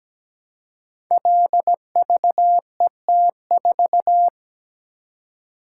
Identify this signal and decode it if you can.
Morse code, heard as a keyed tone, about 17 wpm, 700 Hz. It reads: LVET4
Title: Code Groups - 5 Letter:Numbers - 17wpm